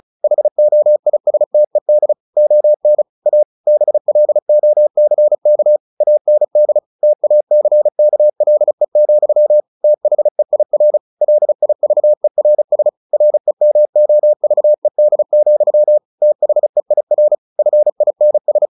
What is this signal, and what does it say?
Morse code, heard as a keyed tone, about 34 wpm, 600 Hz. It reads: HOISTED ON A BLOCK AND TACKLE, THEIR LIVERS REMOVED, THEIR FINS
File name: the old man and the sea / the old man and the sea 1 35wpm